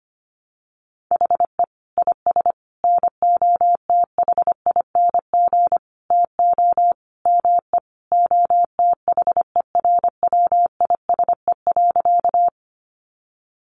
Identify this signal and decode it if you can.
Morse code, heard as a keyed tone, about 25 wpm, 700 Hz. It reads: HE IS NOTHING TO ME OTHERWISE.
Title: Sentences from Top 1200 Words 25wpm